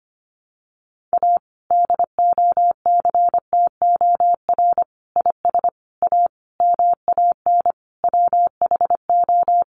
Morse code, heard as a keyed tone, about 25 wpm, 700 Hz. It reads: A DOCTOR IS A MAN WHO
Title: Sentences from Top 1500 Words 25wpm